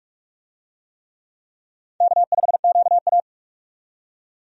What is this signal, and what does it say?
Morse code, heard as a keyed tone, about 42 wpm, 700 Hz. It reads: K5QA
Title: USA Call Signs - 4 Characters - 45wpm